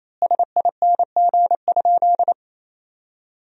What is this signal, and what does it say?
Morse code, heard as a keyed tone, about 28 wpm, 700 Hz. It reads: SING?
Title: 2x Word-Spacing - Sentences from Top 400 Words - 28wpm